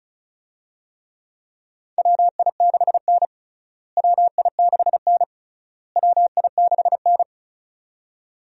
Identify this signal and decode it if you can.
Morse code, heard as a keyed tone, about 34 wpm, 700 Hz. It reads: WI6N WI6N WI6N
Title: USA Call Signs - 4 Characters - Repeated 3x - 35wpm